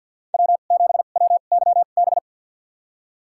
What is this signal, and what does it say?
Morse code, heard as a keyed tone, about 48 wpm, spaced as for 35 wpm, 700 Hz. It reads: W7WYB
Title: USA Call Signs - 5 Characters - 2x Character Spacing - 50wpm